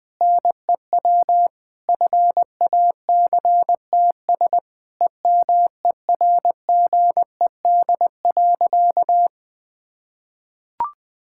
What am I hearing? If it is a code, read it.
Morse code, heard as a keyed tone, about 20 wpm, 700 Hz. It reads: NEW FACTS EMERGED.
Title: Sentences with 5-Word Phrases - 20wpm